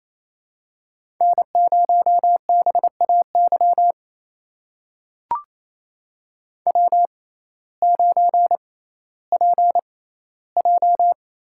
Morse code, heard as a keyed tone, about 28 wpm, spaced as for 15 wpm, 700 Hz. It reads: N0BAY W9PJ
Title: USA Call Signs - 5 Characters - 6x Character Spacing - 28wpm